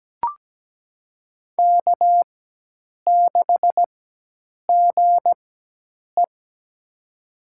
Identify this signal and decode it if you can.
Morse code, heard as a keyed tone, about 17 wpm, spaced as for 8 wpm, 700 Hz. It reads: K6GE
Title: USA Call Signs - 4 Characters - 4x Character Spacing - 17wpm